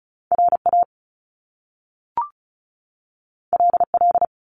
Morse code, heard as a keyed tone, about 36 wpm, 700 Hz. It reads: RA LL
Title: Top Two Letter Combinations in Words 35wpm